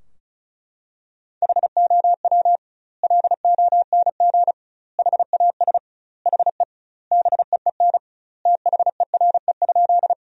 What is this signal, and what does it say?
Morse code, heard as a keyed tone, about 34 wpm, 700 Hz. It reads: HOW LONG HAS HE BEEN THERE?
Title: 2x Word-Spacing - Sentences from Top 100 Words - 35wpm